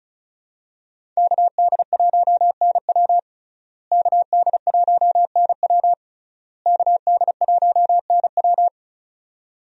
Morse code, heard as a keyed tone, about 36 wpm, 700 Hz. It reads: KD1NW KD1NW KD1NW
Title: USA Call Signs - 5 Characters - Repeated 3x - 35wpm